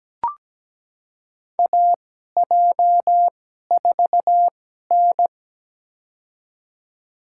Morse code, heard as a keyed tone, about 17 wpm, spaced as for 12 wpm, 700 Hz. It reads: AJ4N
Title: USA Call Signs - 4 Characters - 2x Character Spacing - 17wpm